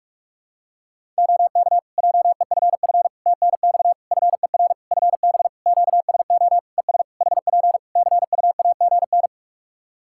Morse code, heard as a keyed tone, about 46 wpm, 700 Hz. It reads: OK JEFF TNX FER FB QSO ES HP CUAGN
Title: Practice QSO - Exchange 4 - Ending - 45wpm